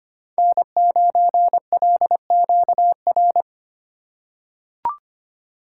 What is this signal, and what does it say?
Morse code, heard as a keyed tone, about 25 wpm, 700 Hz. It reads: N9LQR
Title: USA Call Signs - 5 Characters - 4x Character Spacing - 25wpm